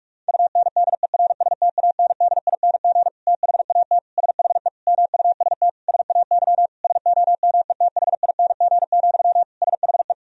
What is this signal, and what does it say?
Morse code, heard as a keyed tone, about 46 wpm, 700 Hz. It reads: UNDERSTANDING THAT SHE MUST SAY SOMETHING, SHE
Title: Sentences from Top 1000 Words - Review Part 3 - 45wpm